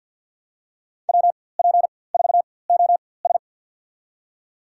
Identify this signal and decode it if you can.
Morse code, heard as a keyed tone, about 52 wpm, spaced as for 23 wpm, 700 Hz. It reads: WP4OS
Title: USA Call Signs - 4x Character Spacing - 50wpm